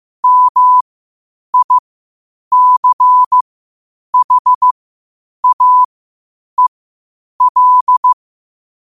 Morse code, heard as a keyed tone, about 15 wpm, spaced as for 8 wpm, 1000 Hz. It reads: MICHAEL
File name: qotdfiles / 8wpm